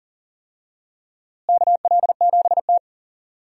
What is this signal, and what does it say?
Morse code, heard as a keyed tone, about 40 wpm, 700 Hz. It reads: KL7T
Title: DXCC-100 Call Signs - 4 Characters - Repeated 2x - 40wpm